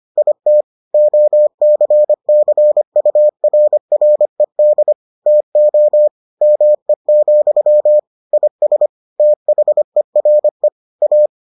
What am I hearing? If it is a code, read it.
Morse code, heard as a keyed tone, about 25 wpm, 600 Hz. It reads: IT OCCURRED TO ME, IS THERE A